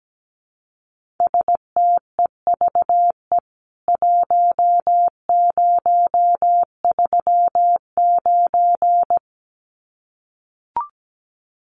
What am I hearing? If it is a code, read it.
Morse code, heard as a keyed tone, about 17 wpm, 700 Hz. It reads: STEVE 1039
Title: CWT Exchange 17wpm